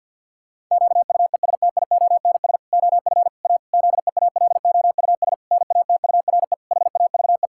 Morse code, heard as a keyed tone, about 46 wpm, 700 Hz. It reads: QUESTIONS OF A ZEALOUS NATURE HAVE
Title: Pangram Challenge v2 - 50wpm